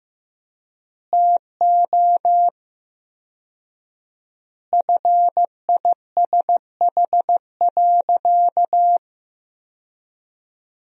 Morse code, heard as a keyed tone, about 15 wpm, 700 Hz. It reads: TO FISH.
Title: 4x Word-Spacing - Sentences from Top 300 Words - 15wpm